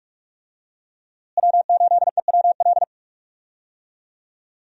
Morse code, heard as a keyed tone, about 46 wpm, 700 Hz. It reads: W8EWL
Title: USA Call Signs - 5 Characters - Repeated 3x - 45wpm